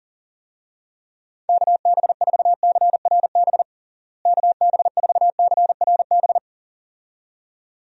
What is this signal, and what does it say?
Morse code, heard as a keyed tone, about 40 wpm, 700 Hz. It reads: KB4CRB KB4CRB
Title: USA Call Signs - 6 Characters - Repeated 2x - 40wpm